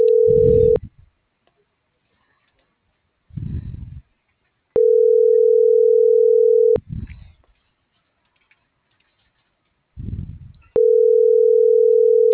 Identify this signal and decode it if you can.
Ringback tone (North America), 440 + 480 Hz.